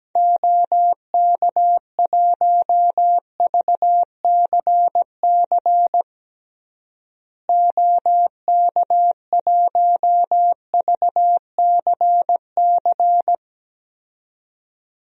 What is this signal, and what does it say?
Morse code, heard as a keyed tone, about 17 wpm, 700 Hz. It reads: OK1VCC OK1VCC
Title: DXCC-100 Call Signs - 6 Characters - Repeated 2x - 17wpm